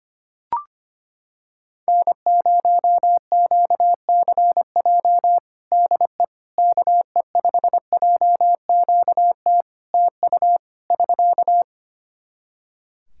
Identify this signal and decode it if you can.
Morse code, heard as a keyed tone, about 25 wpm, 700 Hz. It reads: N0QCJ DE KE5JQT TU <SK>
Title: Practice QSO - Exchange 4 - Ending - 25wpm